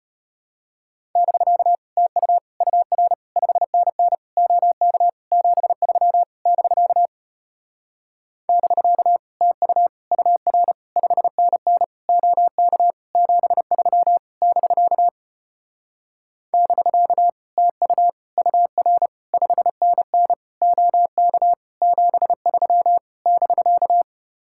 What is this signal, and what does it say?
Morse code, heard as a keyed tone, about 33 wpm, 700 Hz. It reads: <BK> TU UR 5NN OK 73 <BK> <BK> TU UR 5NN OK 73 <BK> <BK> TU UR 5NN OK 73 <BK>